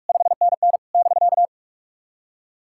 Morse code, heard as a keyed tone, about 43 wpm, 690 Hz. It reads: 5NN <BK>
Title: SOTA QSO Practice 45wpm